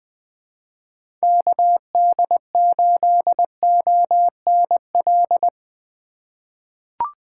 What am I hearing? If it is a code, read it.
Morse code, heard as a keyed tone, about 20 wpm, 700 Hz. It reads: KD8ONL
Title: USA Call Signs - 6 Characters - 4x Character Spacing - 20wpm